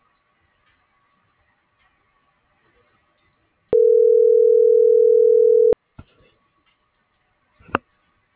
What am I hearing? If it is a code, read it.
Ringback tone (North America), 440 + 480 Hz.